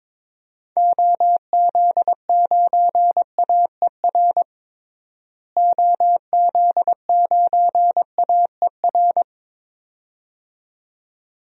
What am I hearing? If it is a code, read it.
Morse code, heard as a keyed tone, about 22 wpm, 700 Hz. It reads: OZ9AER OZ9AER
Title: DXCC-100 Call Signs - 6 Characters - Repeated 2x - 22wpm